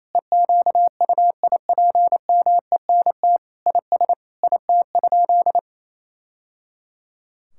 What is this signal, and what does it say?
Morse code, heard as a keyed tone, about 28 wpm, 700 Hz. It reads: EQUIPMENT IS IT?
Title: Sentences from Top 1000 Words - Review Part 5 - 28wpm